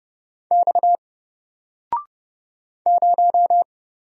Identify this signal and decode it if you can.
Morse code, heard as a keyed tone, about 31 wpm, 700 Hz. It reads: X 0
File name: Single-Letter-Number-30wpm